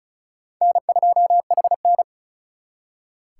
Morse code, heard as a keyed tone, about 48 wpm, spaced as for 42 wpm, 700 Hz. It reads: N2HN
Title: Speed Racing - USA Call Signs 35wpm